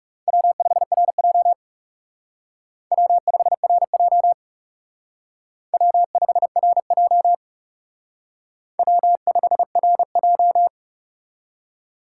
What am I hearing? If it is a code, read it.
Morse code, heard as a keyed tone, about 36 wpm, 700 Hz. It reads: W5RJ W5RJ W5RJ W5RJ
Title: Speed Racing - USA 1x2 Call Signs 30wpm